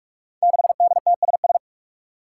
Morse code, heard as a keyed tone, about 44 wpm, 700 Hz. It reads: 6DTSS
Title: Five Challenging Letter-Number Combinations 45wpm